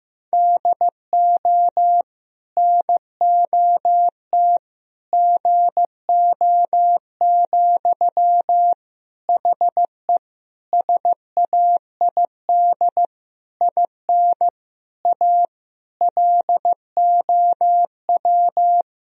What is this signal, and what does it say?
Morse code, heard as a keyed tone, about 15 wpm, 700 Hz. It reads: DO NOT GO, HE SAID IN A LOW
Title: Sentences from Top 1000 Words - Review Part 1 - 15wpm